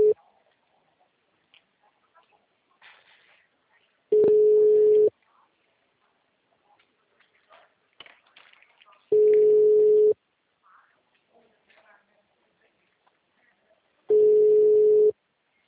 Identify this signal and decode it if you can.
Ringback tone (Europe, 425 Hz), with speech.